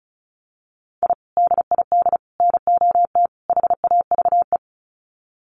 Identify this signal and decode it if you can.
Morse code, heard as a keyed tone, about 34 wpm, 700 Hz. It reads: I DID NOT HAVE
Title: Sets of 4 Words 35wpm